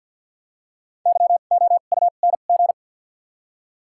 Morse code, heard as a keyed tone, about 53 wpm, spaced as for 37 wpm, 700 Hz. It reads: YOUNG
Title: 2x Character Spacing - Top 500 Words - 50wpm